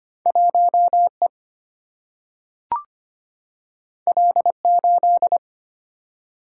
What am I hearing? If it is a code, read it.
Morse code, heard as a keyed tone, about 25 wpm, 700 Hz. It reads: 1E L8
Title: Single-Letter Single-Number 25wpm